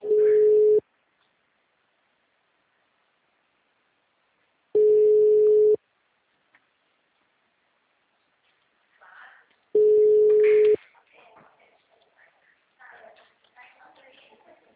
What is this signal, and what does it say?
Ringback tone (Europe, 425 Hz), with speech.
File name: out- -2039- - - .51179